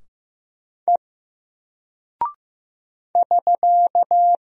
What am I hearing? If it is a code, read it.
Morse code, heard as a keyed tone, about 15 wpm, 700 Hz. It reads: E <SK>
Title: Single Character (Letter:Number:Punctuation:Prosign) Alt 15wpm